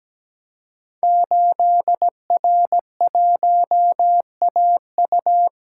Morse code, heard as a keyed tone, about 17 wpm, 700 Hz. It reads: 8R1AU
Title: Code Groups - 5 Letter:Numbers - 17wpm